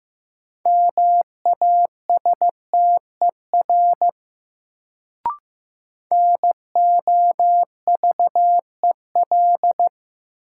Morse code, heard as a keyed tone, about 15 wpm, 700 Hz. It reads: MASTER NOVEL
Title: Sentences from Top 1300 Words - New Word Drill - 15wpm